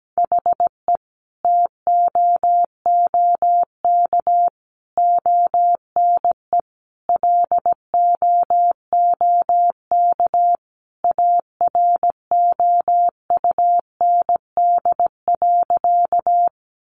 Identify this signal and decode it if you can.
Morse code, heard as a keyed tone, about 17 wpm, 700 Hz. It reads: HE TOOK ONE LOOK AROUND.
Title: Sentences from Top 600 Words 17wpm